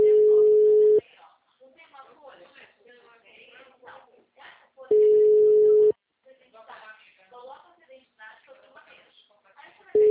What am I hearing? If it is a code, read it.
Ringback tone (Europe, 425 Hz), with speech.